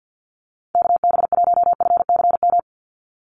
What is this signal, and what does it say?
Morse code, heard as a keyed tone, about 50 wpm, 700 Hz. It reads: KB1FCM
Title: USA Call Signs 50wpm